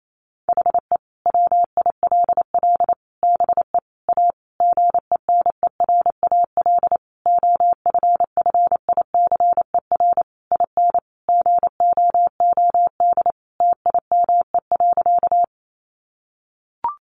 Morse code, heard as a keyed tone, about 28 wpm, 700 Hz. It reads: HE WILL BE A GENERAL OFFICER IN GOOD TIME.
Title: Sentences from Top 1200 Words 28wpm